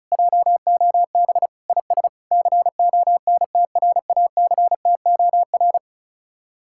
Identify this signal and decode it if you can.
Morse code, heard as a keyed tone, about 46 wpm, 700 Hz. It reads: JOB IS CONTRACTOR